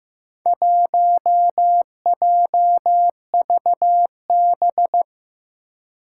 Morse code, heard as a keyed tone, about 15 wpm, 700 Hz. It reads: 1JVB
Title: Four Challenging Letter-Number Combinations 15wpm